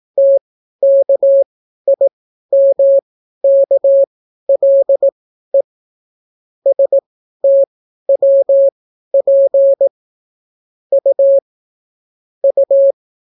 Morse code, heard as a keyed tone, about 18 wpm, spaced as for 12 wpm, 550 Hz. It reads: TKIMKLE STWP U U